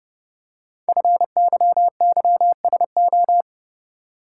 Morse code, heard as a keyed tone, about 30 wpm, 700 Hz. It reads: FYYSO
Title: Code Groups - 5 Letters - 30wpm